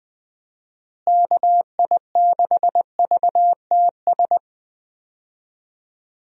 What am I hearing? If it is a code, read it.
Morse code, heard as a keyed tone, about 20 wpm, 700 Hz. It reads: KI6VTS
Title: USA Call Signs - 6 Characters - Repeated 2x - 20wpm